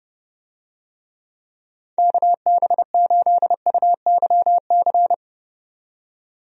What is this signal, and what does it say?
Morse code, heard as a keyed tone, about 31 wpm, 700 Hz. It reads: KB8UYC
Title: USA Call Signs - 6 Characters - 30wpm